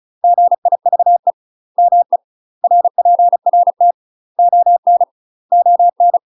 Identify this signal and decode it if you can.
Morse code, heard as a keyed tone, about 36 wpm, 700 Hz. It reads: GIVE ME RPRT ON ON
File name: UK MASTER QSOs TRAINING / 35WPM / UK MASTER QSOs 0013